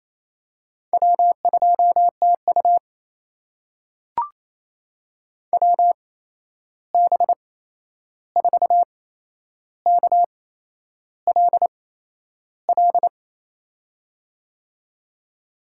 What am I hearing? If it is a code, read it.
Morse code, heard as a keyed tone, about 28 wpm, spaced as for 11 wpm, 700 Hz. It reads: W2TU WB4KLL
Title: USA Call Signs - 8x Character Spacing - 28wpm